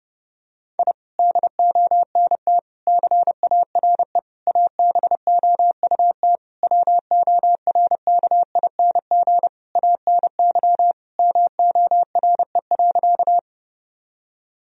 Morse code, heard as a keyed tone, about 28 wpm, 700 Hz. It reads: I DONT CARE ABOUT WORKING ANY MORE.
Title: Sentences from Top 1000 Words - Review Part 4 - 30wpm